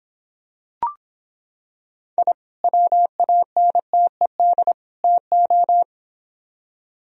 Morse code, heard as a keyed tone, about 26 wpm, 700 Hz. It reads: I WANTED TO